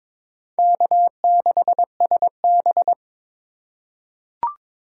Morse code, heard as a keyed tone, about 22 wpm, 700 Hz. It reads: K6SB
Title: USA Call Signs - 3x Character Spacing - 22wpm